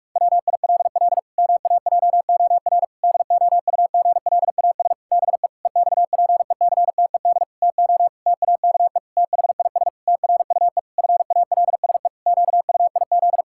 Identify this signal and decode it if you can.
Morse code, heard as a keyed tone, about 46 wpm, 700 Hz. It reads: WILL MAJOR DOUGLAS BE EXPECTED TO TAKE THIS TRUE FALSE QUIZ
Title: Pangram Challenge v2 - 45wpm